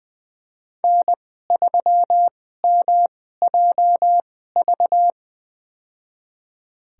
Morse code, heard as a keyed tone, about 20 wpm, spaced as for 15 wpm, 700 Hz. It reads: N3MJV
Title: USA Call Signs - 2x Character Spacing - 20wpm